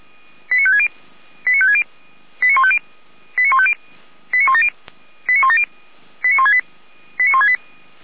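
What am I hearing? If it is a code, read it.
Five-tone selective call (ZVEI), heard as quick runs of five tones, tones of about 70 ms. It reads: 89560 89560 89150 89150 89170 89170 89168 89168
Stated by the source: ZVEI five-tone call 89167 received on an SDR, from the audio server of feuerwehr-orlamuende.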